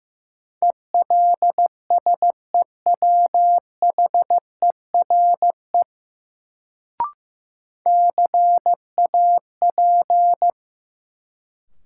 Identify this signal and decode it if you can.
Morse code, heard as a keyed tone, about 15 wpm, 700 Hz. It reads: ELSEWHERE CAP
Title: Sentences from Top 1600 Words - New Word Drill Alt - 15wpm